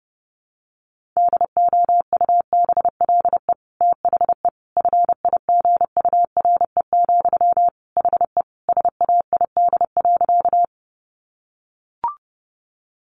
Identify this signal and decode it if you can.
Morse code, heard as a keyed tone, about 28 wpm, 700 Hz. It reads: DOUBLE THE FIGURE, HE SAID.
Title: Sentences from Top 1400 Words 30wpm